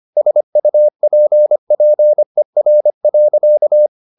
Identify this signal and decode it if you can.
Morse code, heard as a keyed tone, about 25 wpm, 600 Hz. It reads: SUPPER.
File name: the old man and the sea part 1 25wpm